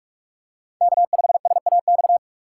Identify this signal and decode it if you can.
Morse code, heard as a keyed tone, about 46 wpm, 700 Hz. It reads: K5SAX
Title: USA Call Signs - 5 Characters - 2x Character Spacing - 45wpm